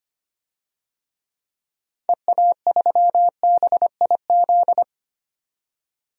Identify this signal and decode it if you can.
Morse code, heard as a keyed tone, about 25 wpm, 700 Hz. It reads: EA3BIZ
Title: DXCC-100 Call Signs - 6 Characters - 25wpm